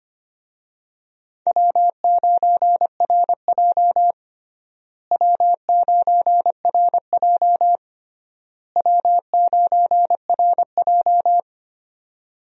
Morse code, heard as a keyed tone, about 25 wpm, 700 Hz. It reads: W9RJ W9RJ W9RJ
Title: USA Call Signs - 4 Characters - Repeated 3x - 25wpm